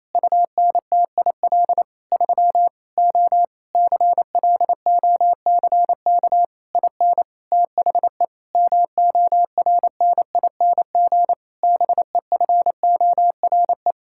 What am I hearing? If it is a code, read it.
Morse code, heard as a keyed tone, about 28 wpm, 700 Hz. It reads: UNTIL 3 O CLOCK IN THE MORNING BEFORE
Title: Pangram Challenge v2 - 28wpm